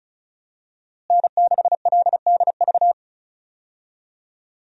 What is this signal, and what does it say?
Morse code, heard as a keyed tone, about 36 wpm, 700 Hz. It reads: N6LDV
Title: USA Call Signs - 5 Characters - Repeated 2x - 35wpm